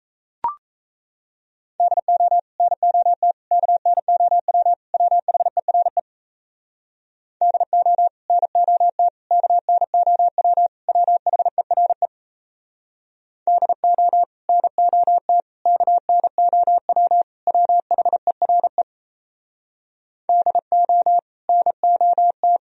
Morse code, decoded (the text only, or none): DO NOT KNOW WHERE DO NOT KNOW WHERE DO NOT KNOW WHERE DO NOT